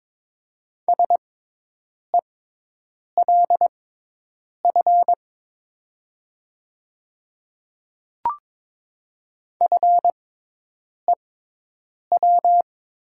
Morse code, heard as a keyed tone, about 22 wpm, spaced as for 8 wpm, 690 Hz. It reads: SELF FEW
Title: 6x Character Spacing - Top 500 Words - 22wpm